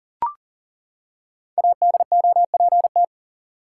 Morse code, heard as a keyed tone, about 40 wpm, 700 Hz. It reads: ADOPT
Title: Five Letter Words - 40wpm